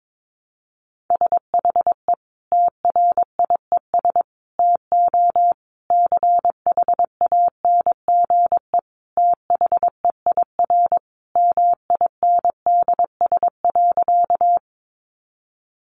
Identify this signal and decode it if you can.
Morse code, heard as a keyed tone, about 22 wpm, 700 Hz. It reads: SHE TRIES TO CHANGE THEIR MINDS.
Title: Easy Phrases 22wpm